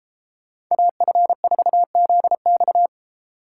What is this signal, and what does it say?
Morse code, heard as a keyed tone, about 35 wpm, 700 Hz. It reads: AF4ZX
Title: Speed Racing - USA 2x2 Call Signs 28wpm